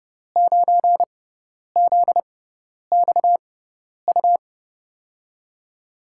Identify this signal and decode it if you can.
Morse code, heard as a keyed tone, about 30 wpm, spaced as for 10 wpm, 700 Hz. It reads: 9ZXU
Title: USA Call Signs - 5 Characters - 6x Character Spacing - 30wpm